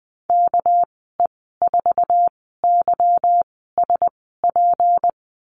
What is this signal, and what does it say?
Morse code, heard as a keyed tone, about 20 wpm, spaced as for 15 wpm, 700 Hz. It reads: KE4YSP
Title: 200 Lesson 32g - Letter k - Call Signs 20wpm